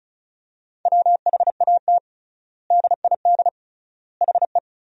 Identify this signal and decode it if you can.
Morse code, heard as a keyed tone, about 34 wpm, 700 Hz. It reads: WHAT DID HE
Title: 3x Word-Spacing - Sentences from Top 200 Words - 35wpm